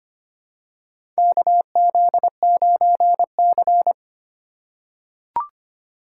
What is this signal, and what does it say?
Morse code, heard as a keyed tone, about 25 wpm, 690 Hz. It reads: KZ9C
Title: USA Call Signs - 4 Characters - 4x Character Spacing - 25wpm